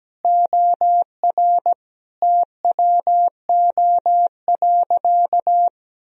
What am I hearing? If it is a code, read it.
Morse code, heard as a keyed tone, about 17 wpm, 700 Hz. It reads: OR TWO.
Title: Sentences from Top 1000 Words - Review Part 1 - 17wpm